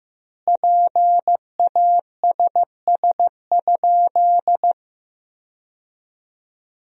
Morse code, heard as a keyed tone, about 15 wpm, 700 Hz. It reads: PASS?
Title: 2x Word-Spacing - Sentences from Top 400 Words - 15wpm